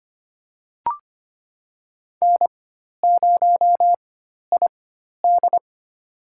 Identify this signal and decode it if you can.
Morse code, heard as a keyed tone, about 25 wpm, spaced as for 12 wpm, 700 Hz. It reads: N0ID
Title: USA Call Signs - 4x Character Spacing - 25wpm